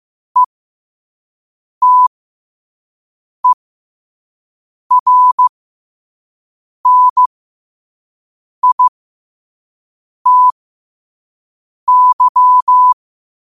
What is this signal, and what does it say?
Morse code, heard as a keyed tone, about 15 wpm, spaced as for 5 wpm, 1000 Hz. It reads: ETERNITY
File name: qotdfiles / 5wpm